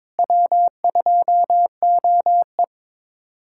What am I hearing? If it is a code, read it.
Morse code, heard as a keyed tone, about 22 wpm, 700 Hz. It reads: W2OE
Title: USA Callsign DE Callsign - 4 Characters - 22wpm